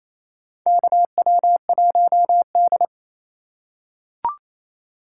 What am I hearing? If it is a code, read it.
Morse code, heard as a keyed tone, about 28 wpm, 700 Hz. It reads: KW1D
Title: USA Call Signs - 4x Character Spacing - 28wpm